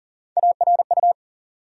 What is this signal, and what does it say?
Morse code, heard as a keyed tone, about 40 wpm, 700 Hz. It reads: ARU
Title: USA Call Sign Postfixes - 3 Letters 40wpm